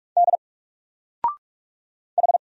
Morse code, heard as a keyed tone, about 44 wpm, 700 Hz. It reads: D H
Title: Single Letter-Number Alt 45wpm